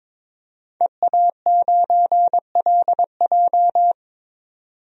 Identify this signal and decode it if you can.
Morse code, heard as a keyed tone, about 22 wpm, 700 Hz. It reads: EA9LJ
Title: DXCC-100 Call Signs - 5 Characters - 22wpm